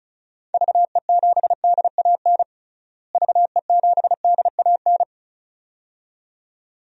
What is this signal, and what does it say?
Morse code, heard as a keyed tone, about 36 wpm, 700 Hz. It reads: VE7DAN VE7DAN
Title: DXCC-100 Call Signs - 6 Characters - Repeated 2x - 35wpm